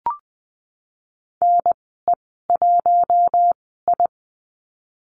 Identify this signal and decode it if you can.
Morse code, heard as a keyed tone, about 20 wpm, spaced as for 15 wpm, 700 Hz. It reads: NE1I
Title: 055 Lesson 10g - Letter r - Call Signs 20wpm